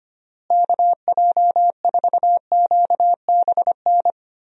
Morse code, heard as a keyed tone, about 25 wpm, 700 Hz. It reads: KJ4QBN
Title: USA Call Signs - 6 Characters - Repeated 2x - 25wpm